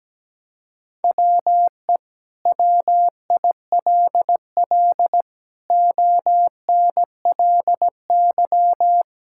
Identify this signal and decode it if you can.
Morse code, heard as a keyed tone, about 17 wpm, 700 Hz. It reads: WE WILL ONLY
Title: Sentences from Top 1000 Words - Review Part 5 - 17wpm